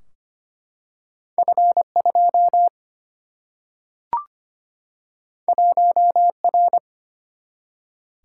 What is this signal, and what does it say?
Morse code, heard as a keyed tone, about 25 wpm, 700 Hz. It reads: F2 1R
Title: Single-Letter Single-Number 25wpm